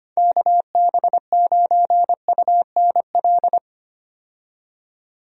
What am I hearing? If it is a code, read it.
Morse code, heard as a keyed tone, about 25 wpm, 700 Hz. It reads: KB9UNL
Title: USA Call Signs - 6 Characters - 25wpm